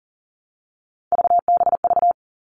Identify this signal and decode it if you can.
Morse code, heard as a keyed tone, about 40 wpm, 700 Hz. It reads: VBV
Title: All Cut Numbers - 3-Digit 40wpm